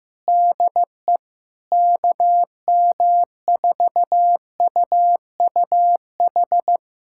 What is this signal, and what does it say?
Morse code, heard as a keyed tone, about 15 wpm, 700 Hz. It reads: DE KM4UUH
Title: USA Callsign DE Callsign - 6 Characters - 15wpm